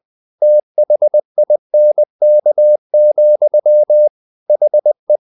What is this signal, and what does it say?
Morse code, heard as a keyed tone, about 20 wpm, 600 Hz. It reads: THINK, HE
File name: the old man and the sea part 2 20wpm